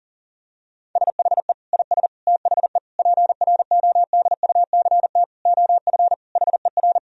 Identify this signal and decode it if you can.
Morse code, heard as a keyed tone, about 40 wpm, 700 Hz. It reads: SHE IS THE PRODUCT OF HER
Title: Sentences from Top 1000 Words - Review Part 2 - 40wpm